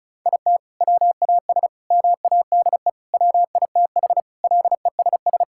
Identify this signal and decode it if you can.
Morse code, heard as a keyed tone, about 36 wpm, 700 Hz. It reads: IT WAS MADE WITH LESS
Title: Sentences from Top 1000 Words - Review Part 5 - 35wpm